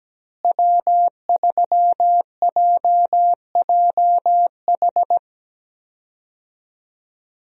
Morse code, heard as a keyed tone, about 17 wpm, 700 Hz. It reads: W3JJH
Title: USA Call Signs - 5 Characters - Repeated 2x - 17wpm